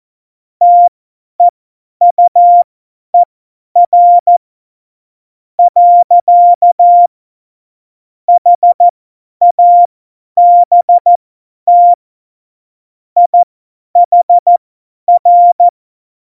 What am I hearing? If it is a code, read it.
Morse code, heard as a keyed tone, about 14 wpm, spaced as for 10 wpm, 700 Hz. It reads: TEUER . HABT IHR
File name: Morsekurs / Witze / Witz 16. Oktober 2018, 50 F70